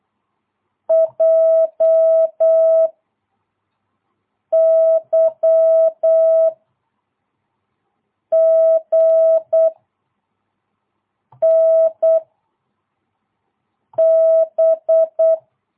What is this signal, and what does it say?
Morse code, heard as a keyed tone, about 8 wpm, 660 Hz. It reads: J Y G N B
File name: jh8cbh / 2022cw